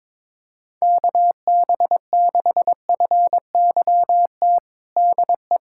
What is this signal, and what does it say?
Morse code, heard as a keyed tone, about 22 wpm, 700 Hz. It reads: KB6FYT DE
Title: USA Callsign DE Callsign - 6 Characters - 22wpm